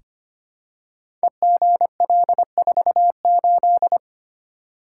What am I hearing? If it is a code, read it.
Morse code, heard as a keyed tone, about 25 wpm, 700 Hz. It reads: EGL48
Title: Code Groups - 5 Letter:Numbers - 25wpm